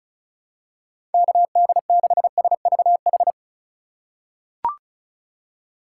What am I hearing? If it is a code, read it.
Morse code, heard as a keyed tone, about 36 wpm, 700 Hz. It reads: KD6SVH
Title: USA Call Signs - 6 Characters - 4x Character Spacing - 35wpm